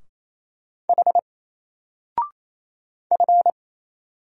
Morse code, heard as a keyed tone, about 27 wpm, 700 Hz. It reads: H F
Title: Single Letter-Number Alt 28wpm